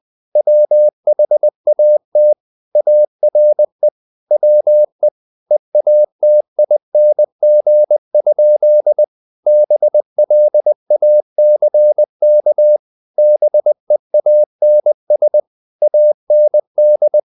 Morse code, heard as a keyed tone, about 20 wpm, 600 Hz. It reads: WHAT ARE WE EATING? BLACK BEANS AND
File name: the old man and the sea / the old man and the sea part 1 20wpm